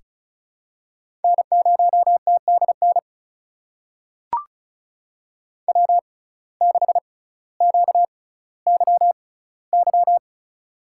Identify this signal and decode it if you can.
Morse code, heard as a keyed tone, about 36 wpm, spaced as for 18 wpm, 700 Hz. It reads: N0TDN W6QYY
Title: USA Call Signs - 5 Characters - 6x Character Spacing - 35wpm